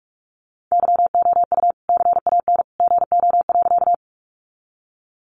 Morse code, heard as a keyed tone, about 44 wpm, 700 Hz. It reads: YOU CAN GO.
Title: Sentences from Top 100 Words 45wpm